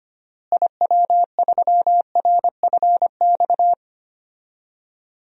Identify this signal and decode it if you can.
Morse code, heard as a keyed tone, about 25 wpm, 700 Hz. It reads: IW3RFX
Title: DXCC-100 Call Signs - 6 Characters - Repeated 2x - 25wpm